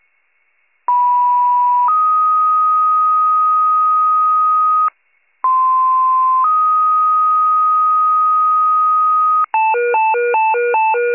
Two-tone page, heard as a one-second tone, then a three-second tone, about 980 then 1250 Hz, about 1040 then 1250 Hz. It is heard from 0.9 to 9.4 s.